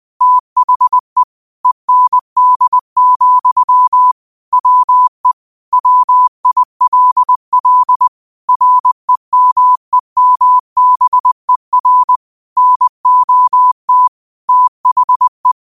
Morse code, heard as a keyed tone, about 20 wpm, 1000 Hz. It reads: THE END, WE WILL REMEMBER NOT THE